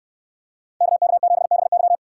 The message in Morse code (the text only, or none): KK7BY